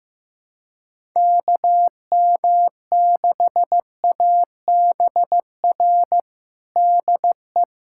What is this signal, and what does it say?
Morse code, heard as a keyed tone, about 15 wpm, 700 Hz. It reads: KM6ABR DE
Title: USA Callsign DE Callsign 15wpm